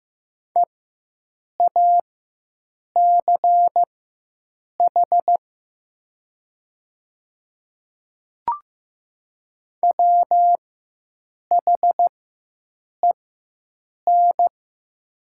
Morse code, heard as a keyed tone, about 15 wpm, spaced as for 7 wpm, 700 Hz. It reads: EACH WHEN
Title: 4x Character Spacing - Top 100 Words - 15wpm